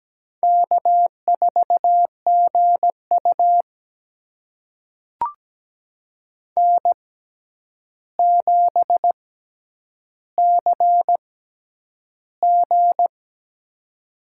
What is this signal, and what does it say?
Morse code, heard as a keyed tone, about 17 wpm, spaced as for 9 wpm, 700 Hz. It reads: K4GU N7CG
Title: USA Call Signs - 4 Characters - 6x Character Spacing - 17wpm